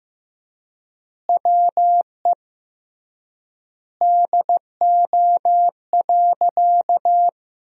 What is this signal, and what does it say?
Morse code, heard as a keyed tone, about 16 wpm, 700 Hz. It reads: WE DO.
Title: 3x Word-Spacing - Sentences from Top 100 Words - 15wpm